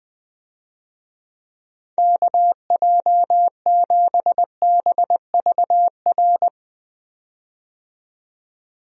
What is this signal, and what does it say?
Morse code, heard as a keyed tone, about 20 wpm, 700 Hz. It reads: KJ7BVR
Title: USA Call Signs - 6 Characters - 20wpm